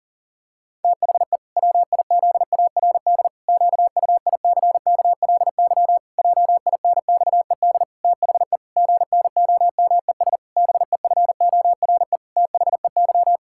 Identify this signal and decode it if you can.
Morse code, heard as a keyed tone, about 40 wpm, 700 Hz. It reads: THE WIZARD QUICKLY JINXED THE GNOMES BEFORE THEY
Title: Pangram Challenge v2 - 40wpm